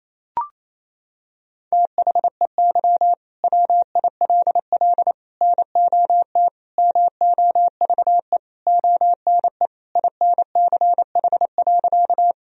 Morse code, heard as a keyed tone, about 28 wpm, 700 Hz. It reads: THEY WILL NOT MOVE ONE INCH.
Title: Sentences from Top 1000 Words - Review Part 3 - 28wpm